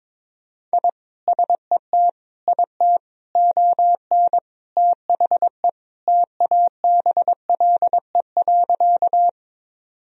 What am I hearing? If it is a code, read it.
Morse code, heard as a keyed tone, about 22 wpm, 700 Hz. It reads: I SET IT ON THE TABLE.
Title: Sentences from Top 1000 Words - Review Part 2 - 22wpm